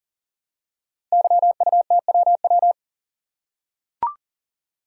Morse code, heard as a keyed tone, about 40 wpm, 700 Hz. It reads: YUTWW
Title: Code Groups - 5 Letters - 40wpm